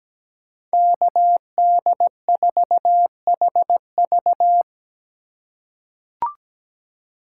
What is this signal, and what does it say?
Morse code, heard as a keyed tone, about 17 wpm, 700 Hz. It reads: KD4HV
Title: USA Call Signs - 5 Characters - 3x Character Spacing - 17wpm